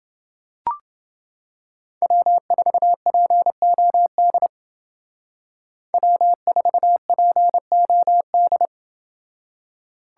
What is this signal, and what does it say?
Morse code, heard as a keyed tone, about 29 wpm, 700 Hz. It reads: W4POD W4POD